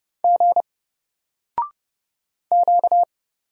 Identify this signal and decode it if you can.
Morse code, heard as a keyed tone, about 30 wpm, 700 Hz. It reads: G Q
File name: Single-Letter-Number-30wpm